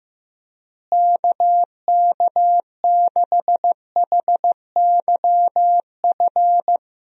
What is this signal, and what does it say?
Morse code, heard as a keyed tone, about 15 wpm, 700 Hz. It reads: KK6HYF
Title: USA Call Signs - 6 Characters - Repeated 3x - 15wpm